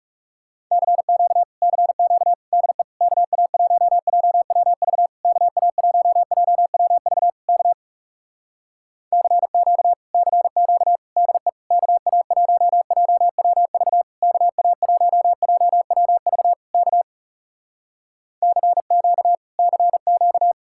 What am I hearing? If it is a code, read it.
Morse code, heard as a keyed tone, about 38 wpm, 700 Hz. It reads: CQ CQ DE KA1JWV KA1JWV K CQ CQ DE KA1JWV KA1JWV K CQ CQ
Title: Speed-Racing Common QSO Phrases Encore - 30wpm